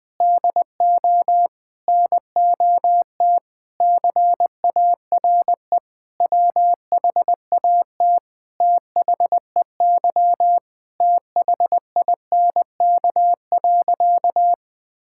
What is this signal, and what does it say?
Morse code, heard as a keyed tone, about 20 wpm, 700 Hz. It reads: DO NOT CARE WHAT THEY THINK.
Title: Sentences with 5-Word Phrases - 20wpm